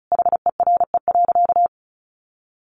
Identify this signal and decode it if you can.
Morse code, heard as a keyed tone, about 34 wpm, 700 Hz. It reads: HERE.
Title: Sentences from Top 1300 Words 35wpm